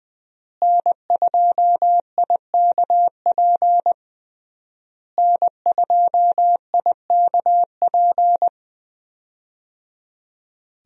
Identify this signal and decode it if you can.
Morse code, heard as a keyed tone, about 20 wpm, 700 Hz. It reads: N2IKP N2IKP
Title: USA Call Signs - 5 Characters - Repeated 3x - 20wpm